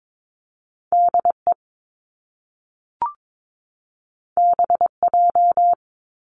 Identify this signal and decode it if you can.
Morse code, heard as a keyed tone, about 22 wpm, 700 Hz. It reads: DE BJ
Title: Top Two Letter Combinations in Words 22wpm